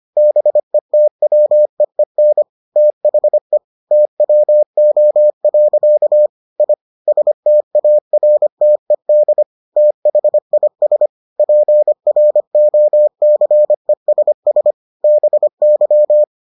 Morse code, heard as a keyed tone, about 25 wpm, 600 Hz. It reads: BETWEEN THE TWO. I STARTED THIS PROCESS BY